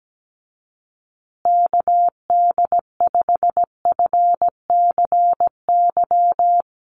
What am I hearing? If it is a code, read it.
Morse code, heard as a keyed tone, about 17 wpm, 700 Hz. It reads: KD5FCY
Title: USA Call Signs Repeated 2x 17wpm